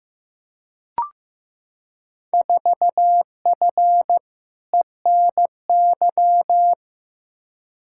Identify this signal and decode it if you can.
Morse code, heard as a keyed tone, about 15 wpm, 700 Hz. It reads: 4F ENY
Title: Field Day Exchanges - 15wpm